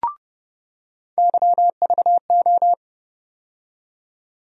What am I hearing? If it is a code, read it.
Morse code, heard as a keyed tone, about 31 wpm, 700 Hz. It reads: YVO
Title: USA Call Sign Postfixes - 3 Letters 30wpm